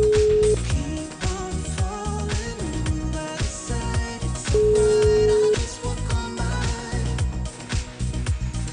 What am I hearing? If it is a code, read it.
Ringback tone (Europe, 425 Hz).